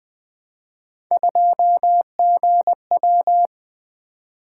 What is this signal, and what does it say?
Morse code, heard as a keyed tone, about 20 wpm, 700 Hz. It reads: 2GW
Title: Three Letter-Number Combinations - 20wpm